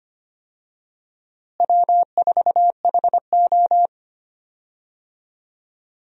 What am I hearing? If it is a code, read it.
Morse code, heard as a keyed tone, about 25 wpm, 700 Hz. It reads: W4HO
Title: USA Call Signs - 4 Characters - Repeated 2x - 25wpm